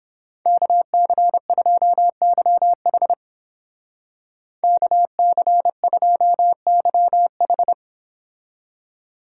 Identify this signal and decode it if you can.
Morse code, heard as a keyed tone, about 29 wpm, 700 Hz. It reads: KC2YH KC2YH